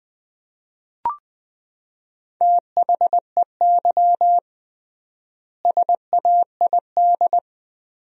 Morse code, heard as a keyed tone, about 20 wpm, 700 Hz. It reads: THEY SAID
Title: 3x Word-Spacing - Sentences from Top 100 Words - 20wpm